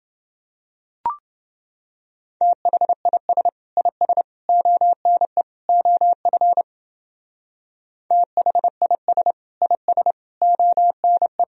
Morse code, heard as a keyed tone, about 29 wpm, 700 Hz. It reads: THIS IS ONE OF THIS IS ONE